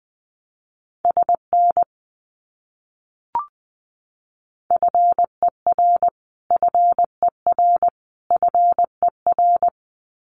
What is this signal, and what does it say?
Morse code, heard as a keyed tone, about 20 wpm, 700 Hz. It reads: SN FER FER FER
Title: Abbreviations Repeated 3x - 20wpm